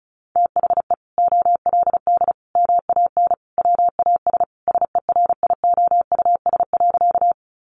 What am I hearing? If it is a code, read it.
Morse code, heard as a keyed tone, about 34 wpm, 700 Hz. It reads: THE OLD MAN WAS SERIOUS.
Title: Sentences from Top 1100 Words 35wpm